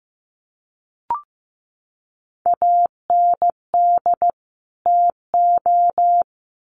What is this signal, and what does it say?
Morse code, heard as a keyed tone, about 15 wpm, 700 Hz. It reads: AND TO
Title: Sets of 2 Words 15wpm